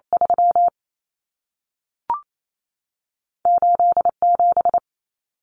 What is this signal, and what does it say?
Morse code, heard as a keyed tone, about 28 wpm, spaced as for 3 wpm, 700 Hz. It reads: 38 7
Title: Common Cut Numbers - 2-Digit 28wpm